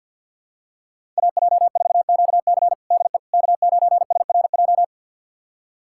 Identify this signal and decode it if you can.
Morse code, heard as a keyed tone, about 48 wpm, 700 Hz. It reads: AJ4QC DE K9SRJ
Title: USA Callsign DE Callsign - 5 Characters - 50wpm